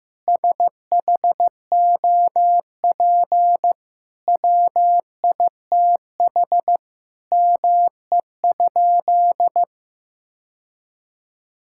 Morse code, heard as a keyed tone, about 15 wpm, 700 Hz. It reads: SHOP WITH ME?
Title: Sentences from Top 1600 Words 15wpm